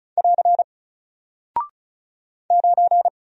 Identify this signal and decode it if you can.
Morse code, heard as a keyed tone, about 36 wpm, 700 Hz. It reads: <AR> 9
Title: Single Character (Letter:Number:Punctuation:Prosign) Alt 35wpm